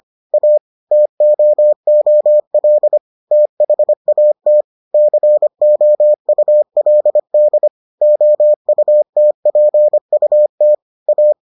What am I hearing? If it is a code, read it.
Morse code, heard as a keyed tone, about 25 wpm, 600 Hz. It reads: A TOOL THAT COULD OUTPUT A